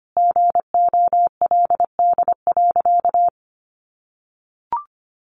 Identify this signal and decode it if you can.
Morse code, heard as a keyed tone, about 25 wpm, 700 Hz. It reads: GOLD.
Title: Sentences from Top 1500 Words 25wpm